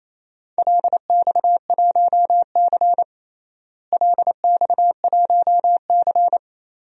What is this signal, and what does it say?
Morse code, heard as a keyed tone, about 28 wpm, 700 Hz. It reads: LX1C LX1C
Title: DXCC-100 Call Signs - 4 Characters - Repeated 2x - 28wpm